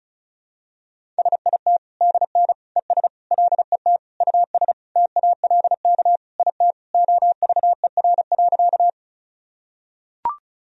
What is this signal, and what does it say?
Morse code, heard as a keyed tone, about 34 wpm, 700 Hz. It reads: SIT DN ES LET US TALK IT OVER.
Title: Sentences from Top 500 Words and Abbreviations 35wpm